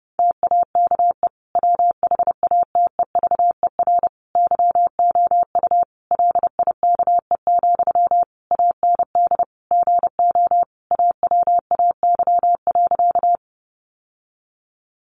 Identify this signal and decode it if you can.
Morse code, heard as a keyed tone, about 30 wpm, 700 Hz. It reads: TAKE WHATEVER YOU LIKE, AND GO AWAY.
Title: Sentences from Top 1100 Words 30wpm